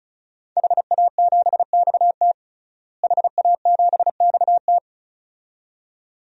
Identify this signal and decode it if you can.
Morse code, heard as a keyed tone, about 34 wpm, 700 Hz. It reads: HA7XT HA7XT
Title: DXCC-100 Call Signs - 5 Characters - Repeated 2x - 35wpm